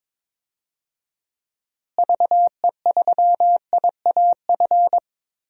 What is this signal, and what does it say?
Morse code, heard as a keyed tone, about 22 wpm, 700 Hz. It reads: VE3IAF
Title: DXCC-100 Call Signs - 6 Characters - Repeated 2x - 22wpm